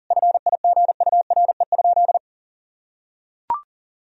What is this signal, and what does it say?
Morse code, heard as a keyed tone, about 40 wpm, 700 Hz. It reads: FIGURE?
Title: Sentences from Top 1000 Words - Review Part 2 - 40wpm